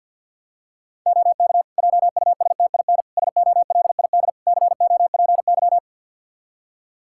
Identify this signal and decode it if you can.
Morse code, heard as a keyed tone, about 46 wpm, 700 Hz. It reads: OK JUSTIN SOLID COPY
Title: Practice QSO - Exchange 3 - More Info - 50wpm